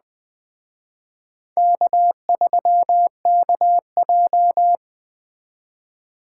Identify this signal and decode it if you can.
Morse code, heard as a keyed tone, about 20 wpm, 700 Hz. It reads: K3KJ